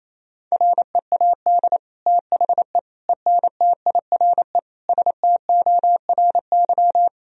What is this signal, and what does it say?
Morse code, heard as a keyed tone, about 27 wpm, 700 Hz. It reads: READ THE ENTIRE STORY
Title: Sentences from Top 1000 Words - Review Part 4 - 28wpm